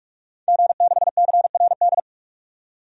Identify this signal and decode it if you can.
Morse code, heard as a keyed tone, about 41 wpm, 700 Hz. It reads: G6CRD
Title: DXCC-100 Call Signs - 5 Characters - 45wpm